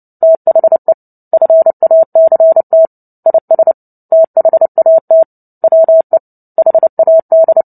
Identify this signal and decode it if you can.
Morse code, heard as a keyed tone, about 30 wpm, 650 Hz. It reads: THE FACT IS THAT WE HAD
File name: Book Two / 03-Chapter 3 - 29 WPM